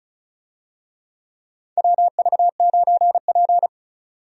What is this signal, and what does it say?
Morse code, heard as a keyed tone, about 34 wpm, 700 Hz. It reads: WV9P
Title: USA Call Signs - 4 Characters - 35wpm